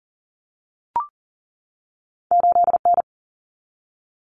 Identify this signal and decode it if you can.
Morse code, heard as a keyed tone, about 40 wpm, 700 Hz. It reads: 8 N
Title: Common Cut Numbers - 2-Digit 40wpm